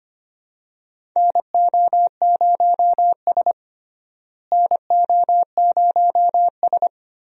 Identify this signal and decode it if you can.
Morse code, heard as a keyed tone, about 25 wpm, 700 Hz. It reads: NO0S NO0S
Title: USA Call Signs - 4 Characters - Repeated 3x - 25wpm